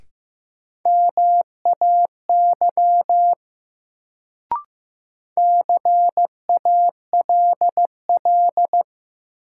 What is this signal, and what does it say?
Morse code, heard as a keyed tone, about 15 wpm, 700 Hz. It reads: MAY CALL
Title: Sentences from Top 100 Words - New Word Drill - 15wpm